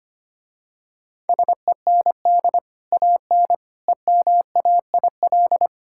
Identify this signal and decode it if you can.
Morse code, heard as a keyed tone, about 25 wpm, 700 Hz. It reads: SEND AN EMAIL
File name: Easy Sentences with Hints - 25wpm